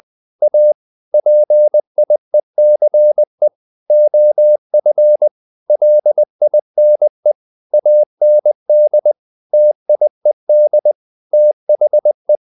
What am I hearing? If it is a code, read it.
Morse code, heard as a keyed tone, about 20 wpm, 600 Hz. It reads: A PIECE OF LINE AND TIED THE
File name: the old man and the sea part 2 20wpm